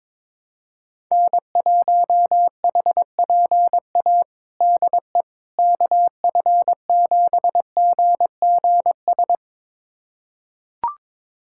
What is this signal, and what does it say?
Morse code, heard as a keyed tone, about 22 wpm, 700 Hz. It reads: N1HPA DE KF7GGS
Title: USA Callsign DE Callsign 22wpm